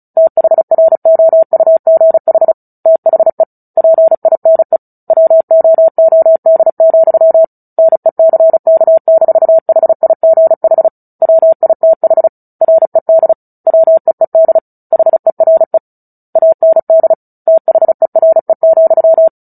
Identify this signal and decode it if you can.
Morse code, heard as a keyed tone, about 34 wpm, 650 Hz. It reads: THROUGH THE PINE WOOD, NECK-HIGH WITH RED WEED HERE AND THERE,
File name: Book Two / 09-Chapter 9 - 35 WPM